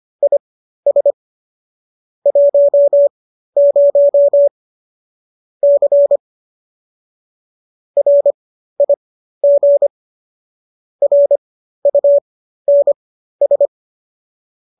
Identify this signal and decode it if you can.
Morse code, heard as a keyed tone, about 25 wpm, spaced as for 13 wpm, 570 Hz. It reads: IS 10 C RIG RUNS